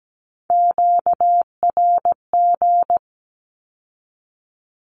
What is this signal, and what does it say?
Morse code, heard as a keyed tone, about 17 wpm, 700 Hz. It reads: QRG
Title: Q-Codes with Spoken Meaning - 17wpm